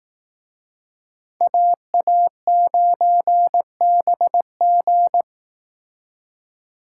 Morse code, heard as a keyed tone, about 19 wpm, 700 Hz. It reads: AA9BG